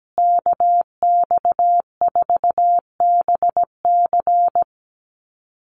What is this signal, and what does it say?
Morse code, heard as a keyed tone, about 17 wpm, 700 Hz. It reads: KX4BC
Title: USA Call Signs Repeated 2x 17wpm